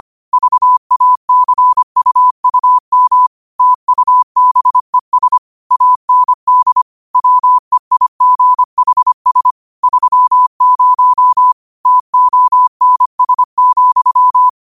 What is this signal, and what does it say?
Morse code, heard as a keyed tone, about 25 wpm, 1000 Hz. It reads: VACUUM TUBES AND WEIGHS 30 TONS,
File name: qotdfiles / 25wpm